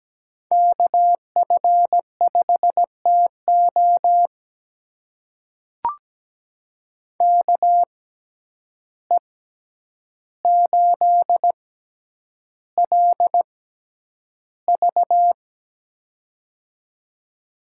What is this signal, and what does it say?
Morse code, heard as a keyed tone, about 17 wpm, spaced as for 9 wpm, 700 Hz. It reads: KF5TO KE8LV
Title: USA Call Signs - 5 Characters - 6x Character Spacing - 17wpm